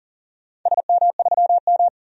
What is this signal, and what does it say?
Morse code, heard as a keyed tone, about 41 wpm, 700 Hz. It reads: SM3M
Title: DXCC-100 Call Signs - 4 Characters - 40wpm